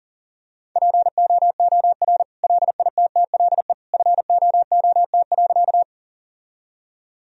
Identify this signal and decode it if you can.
Morse code, heard as a keyed tone, about 40 wpm, 700 Hz. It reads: POOR LITTLE FOOT.
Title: Sentences from Top 1000 Words - Review Part 3 - 40wpm